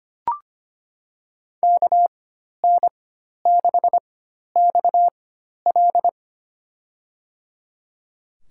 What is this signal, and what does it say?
Morse code, heard as a keyed tone, about 25 wpm, spaced as for 12 wpm, 700 Hz. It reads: KN6XL
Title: USA Call Signs - 4x Character Spacing - 25wpm